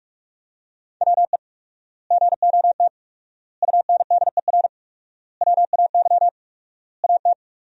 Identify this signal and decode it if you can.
Morse code, heard as a keyed tone, about 46 wpm, 700 Hz. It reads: WE GOT UNDER WAY AT
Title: 4x Word-Spacing - Sentences from Top 300 Words - 45wpm